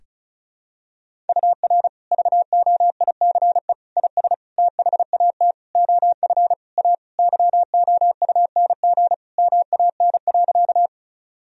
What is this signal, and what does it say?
Morse code, heard as a keyed tone, about 34 wpm, 700 Hz. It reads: UR VOICE IS THAT OF A YOUNG MAN.
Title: Sentences from Top 500 Words and Abbreviations 35wpm